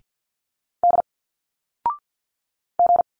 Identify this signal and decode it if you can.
Morse code, heard as a keyed tone, about 50 wpm, 700 Hz. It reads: D G
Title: Single Letters 50wpm